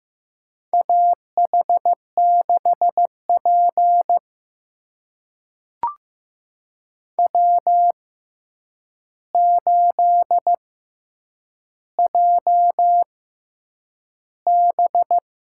Morse code, heard as a keyed tone, about 15 wpm, spaced as for 8 wpm, 700 Hz. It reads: AH6P W8JB
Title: USA Call Signs - 4 Characters - 6x Character Spacing - 15wpm